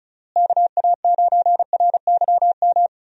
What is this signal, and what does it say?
Morse code, heard as a keyed tone, about 34 wpm, 700 Hz. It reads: KA9RYM
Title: USA Call Signs - 6 Characters - Repeated 3x - 35wpm